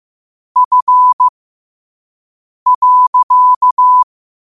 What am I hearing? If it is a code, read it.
Morse code, heard as a keyed tone, about 15 wpm, spaced as for 5 wpm, 1000 Hz. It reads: F.
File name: qotdfiles / 5wpm